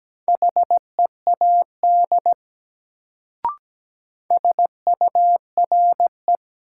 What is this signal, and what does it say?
Morse code, heard as a keyed tone, about 17 wpm, 700 Hz. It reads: HEAD SURE
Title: Sets of 3 Words - New Word Drill - 17wpm